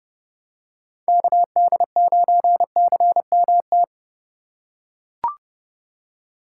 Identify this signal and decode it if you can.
Morse code, heard as a keyed tone, about 29 wpm, 700 Hz. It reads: KD9CMT